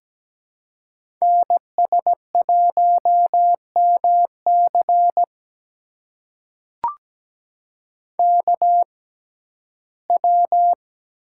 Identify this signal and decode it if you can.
Morse code, heard as a keyed tone, about 17 wpm, spaced as for 12 wpm, 700 Hz. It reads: NS1MC KW
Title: USA Call Signs - 5 Characters - 6x Character Spacing - 17wpm